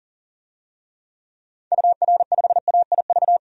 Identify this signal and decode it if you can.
Morse code, heard as a keyed tone, about 37 wpm, 700 Hz. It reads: UR5AIV
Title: DXCC-100 Call Signs - 6 Characters - 40wpm